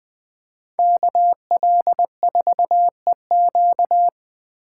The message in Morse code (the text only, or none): KL4EQ